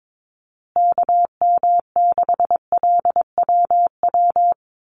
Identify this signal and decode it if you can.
Morse code, heard as a keyed tone, about 22 wpm, 700 Hz. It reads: KM6LWW
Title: USA Call Signs Repeated 2x 22wpm